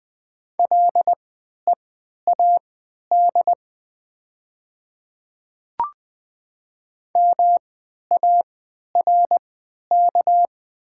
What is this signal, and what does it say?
Morse code, heard as a keyed tone, about 20 wpm, spaced as for 11 wpm, 700 Hz. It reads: LEAD MARK
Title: 3x Character Spacing - Top 500 Words - 20wpm